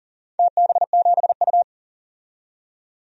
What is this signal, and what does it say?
Morse code, heard as a keyed tone, about 38 wpm, 700 Hz. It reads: TB7U
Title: DXCC-100 Call Signs - 4 Characters - 40wpm